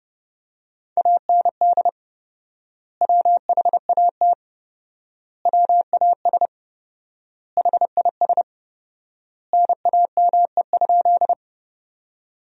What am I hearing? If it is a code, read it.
Morse code, heard as a keyed tone, about 30 wpm, 700 Hz. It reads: AND WHAT WAS HIS NAME?
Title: 4x Word-Spacing - Sentences from Top 200 Words - 30wpm